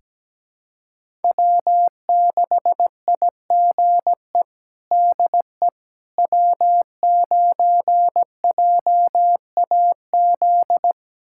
Morse code, heard as a keyed tone, about 17 wpm, 700 Hz. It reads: W6IGE DE W9JAZ
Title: USA Callsign DE Callsign - 5 Characters - 17wpm